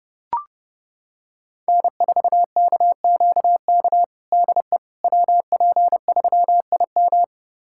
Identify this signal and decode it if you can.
Morse code, heard as a keyed tone, about 32 wpm, 700 Hz. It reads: N4KQK DE WP3IM
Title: USA Callsign DE Callsign - 5 Characters - 30wpm